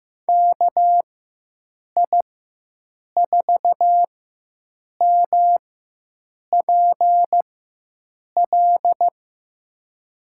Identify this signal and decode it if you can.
Morse code, heard as a keyed tone, about 15 wpm, spaced as for 7 wpm, 700 Hz. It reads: KI4MPL
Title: USA Call Signs - 6 Characters - 4x Character Spacing - 15wpm